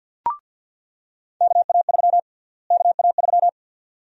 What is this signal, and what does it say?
Morse code, heard as a keyed tone, about 47 wpm, 700 Hz. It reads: KA3 KA3
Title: USA Call Sign Prefixes - Repeated 2x - 50wpm